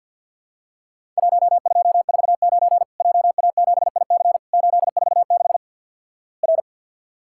Morse code, heard as a keyed tone, about 48 wpm, 700 Hz. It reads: 1249 JA7IC 846 R
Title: Worked All Europe DX Contest - QTC Exchange with Repeats - 50wpm